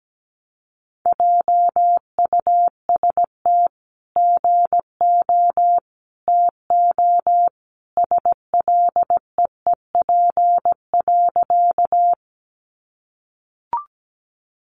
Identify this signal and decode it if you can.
Morse code, heard as a keyed tone, about 17 wpm, 700 Hz. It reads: JUST GO TO SLEEP.
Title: Sentences from Top 500 Words 17wpm